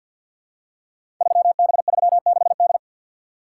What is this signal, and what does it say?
Morse code, heard as a keyed tone, about 46 wpm, 700 Hz. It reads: 3B36D
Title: Code Groups - 5 Letter:Numbers - 50wpm